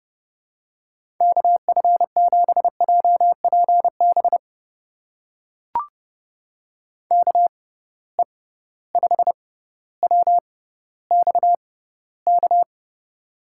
Morse code, heard as a keyed tone, about 32 wpm, spaced as for 16 wpm, 700 Hz. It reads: KF7JPB KE5WXK
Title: USA Call Signs - 6 Characters - 6x Character Spacing - 30wpm